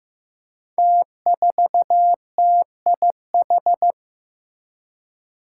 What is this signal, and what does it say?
Morse code, heard as a keyed tone, about 15 wpm, 700 Hz. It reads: T4TIH
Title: Five Challenging Letter-Number Combinations 15wpm